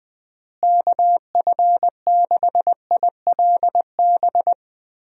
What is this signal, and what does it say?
Morse code, heard as a keyed tone, about 20 wpm, 700 Hz. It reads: KF6ILB
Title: USA Call Signs - 6 Characters - Repeated 2x - 20wpm